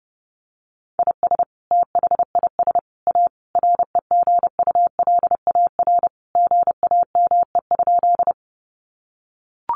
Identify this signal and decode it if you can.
Morse code, heard as a keyed tone, about 30 wpm, 700 Hz. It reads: IS THIS A REGULAR GAME?
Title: Sentences from Top 900 Words 30wpm